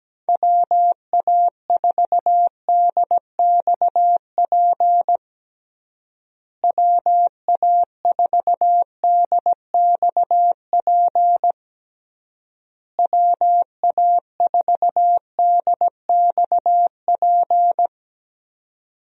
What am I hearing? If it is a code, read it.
Morse code, heard as a keyed tone, about 17 wpm, 700 Hz. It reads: WA4DXP WA4DXP WA4DXP
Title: USA Call Signs - 6 Characters - Repeated 3x - 17wpm